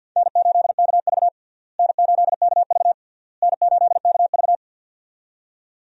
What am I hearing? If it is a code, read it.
Morse code, heard as a keyed tone, about 51 wpm, 700 Hz. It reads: N8KV N8KV N8KV
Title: USA Call Signs - 4 Characters - Repeated 3x - 50wpm